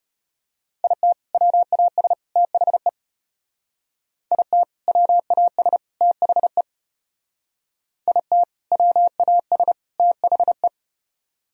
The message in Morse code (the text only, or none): IT WAS THE IT WAS THE IT WAS THE